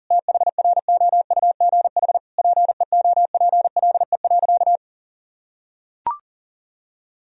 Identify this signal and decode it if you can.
Morse code, heard as a keyed tone, about 40 wpm, 700 Hz. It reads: THROUGH PEOPLE.
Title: Sentences from Top 200 Words and Abbreviations 40wpm